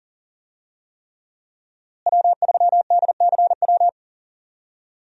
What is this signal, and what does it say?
Morse code, heard as a keyed tone, about 40 wpm, 700 Hz. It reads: W3DCW
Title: USA Call Signs - 5 Characters - Repeated 3x - 40wpm